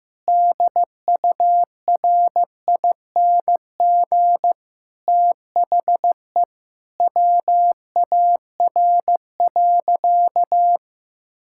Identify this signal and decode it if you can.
Morse code, heard as a keyed tone, about 16 wpm, 700 Hz. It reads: DURING THE WAR.
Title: Sentences from Top 1000 Words - Review Part 6 - 15wpm